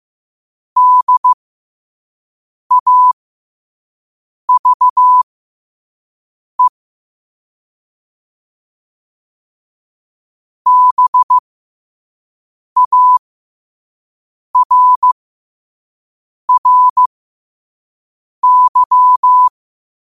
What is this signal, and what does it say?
Morse code, heard as a keyed tone, about 15 wpm, spaced as for 5 wpm, 1000 Hz. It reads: DAVE BARRY
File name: qotdfiles / 5wpm